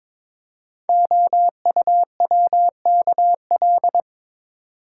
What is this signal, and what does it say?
Morse code, heard as a keyed tone, about 22 wpm, 700 Hz. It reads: OUWKL
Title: Code Groups - 5 Letters - 22wpm